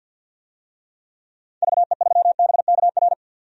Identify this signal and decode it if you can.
Morse code, heard as a keyed tone, about 50 wpm, 700 Hz. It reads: VE3BKR
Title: DXCC-100 Call Signs - 6 Characters - 50wpm